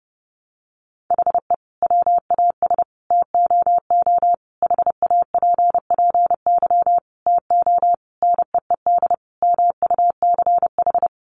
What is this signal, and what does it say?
Morse code, heard as a keyed tone, about 28 wpm, 700 Hz. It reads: HE WAS TOO HAPPY TO NEED MUCH
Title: Sentences from Top 1000 Words 30wpm